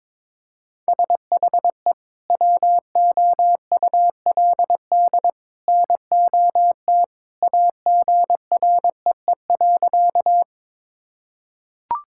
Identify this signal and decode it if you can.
Morse code, heard as a keyed tone, about 22 wpm, 700 Hz. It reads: SHE WOULD NOT AGREE.
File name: practice / 22wpm / Sentences with 2-Word Phrases - 22wpm